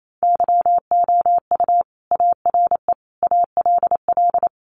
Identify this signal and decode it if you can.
Morse code, heard as a keyed tone, about 28 wpm, 700 Hz. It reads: YOU ARE ALL
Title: Sentences from Top 200 Words 28wpm